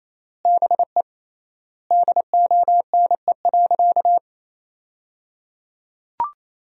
Morse code, heard as a keyed tone, about 28 wpm, 700 Hz. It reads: BE DONE.
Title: 3x Word-Spacing - Sentences from Top 500 Words - 28wpm